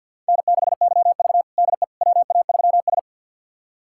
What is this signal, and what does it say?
Morse code, heard as a keyed tone, about 49 wpm, 700 Hz. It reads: N6YV DE WA3S
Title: USA Callsign DE Callsign - 4 Characters - 50wpm